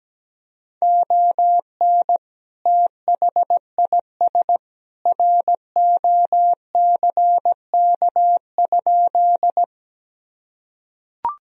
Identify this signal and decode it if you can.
Morse code, heard as a keyed tone, about 17 wpm, 700 Hz. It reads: ON THIS ROCK?
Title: Sentences from Top 1000 Words - Review Part 2 - 17wpm